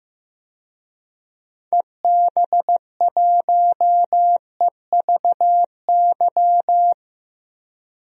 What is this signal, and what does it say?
Morse code, heard as a keyed tone, about 15 wpm, 700 Hz. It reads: EB1EVY
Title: DXCC-100 Call Signs - 6 Characters - Repeated 3x - 15wpm